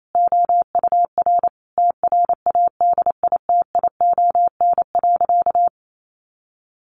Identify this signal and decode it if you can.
Morse code, heard as a keyed tone, about 28 wpm, 700 Hz. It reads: OUR TRADITION.
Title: Sentences from Top 1500 Words 28wpm